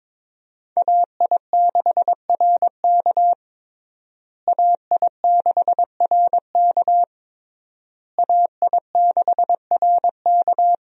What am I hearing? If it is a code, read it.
Morse code, heard as a keyed tone, about 22 wpm, 700 Hz. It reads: AI6RK AI6RK AI6RK
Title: USA Call Signs - 5 Characters - Repeated 3x - 22wpm